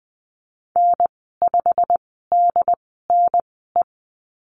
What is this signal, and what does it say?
Morse code, heard as a keyed tone, about 19 wpm, spaced as for 14 wpm, 700 Hz. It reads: N5DNE
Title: 161 Lesson 26g - Letter g - Call Signs 20wpm